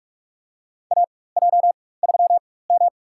Morse code, heard as a keyed tone, about 45 wpm, spaced as for 21 wpm, 700 Hz. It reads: AJ3M
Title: USA Call Signs - 4x Character Spacing - 45wpm